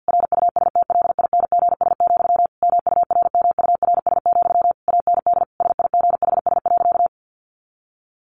Morse code, heard as a keyed tone, about 50 wpm, 700 Hz. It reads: RUSTLINGS, MURMURS, AND SIGHS.
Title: A Sound Of Thunder 50wpm